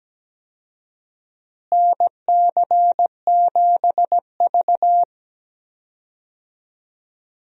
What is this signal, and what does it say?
Morse code, heard as a keyed tone, about 17 wpm, 700 Hz. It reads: NC7V
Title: USA Call Signs - 4 Characters - Repeated 2x - 17wpm